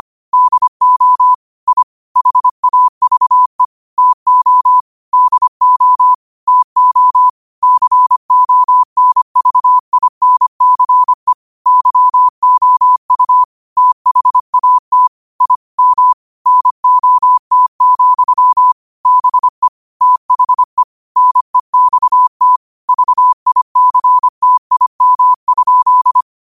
Morse code, heard as a keyed tone, about 25 wpm, 1000 Hz. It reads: DO I HAVE TO DO TO CONVINCE YOU THAT I M NOT, BE THE NEXT VICTIM?
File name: qotdfiles / 25wpm